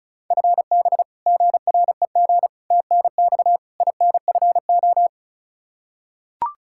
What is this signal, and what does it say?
Morse code, heard as a keyed tone, about 36 wpm, 700 Hz. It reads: FB GREG TNX INFO
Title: Practice QSO - Exchange 3 - More Info - 35wpm